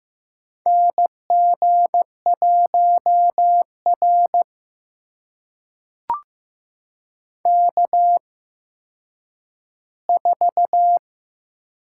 Morse code, heard as a keyed tone, about 15 wpm, spaced as for 9 wpm, 700 Hz. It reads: NG1R K4
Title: USA Call Signs - 4 Characters - 8x Character Spacing - 15wpm